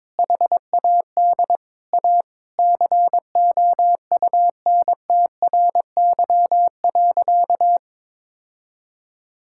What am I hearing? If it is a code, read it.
Morse code, heard as a keyed tone, about 22 wpm, 700 Hz. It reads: HAD A COUNTRY.
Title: Sentences from Top 1600 Words 22wpm